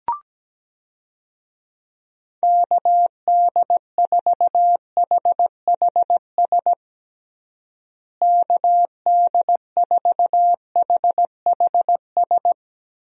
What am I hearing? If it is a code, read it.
Morse code, heard as a keyed tone, about 17 wpm, 700 Hz. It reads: KD4HHS KD4HHS
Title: USA Call Signs - 6 Characters - Repeated 2x - 17wpm